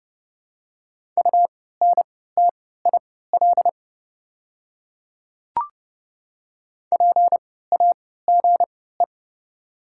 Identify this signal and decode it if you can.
Morse code, heard as a keyed tone, about 30 wpm, spaced as for 17 wpm, 700 Hz. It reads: UNTIL PAGE
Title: 3x Character Spacing - Top 300 Words - 30wpm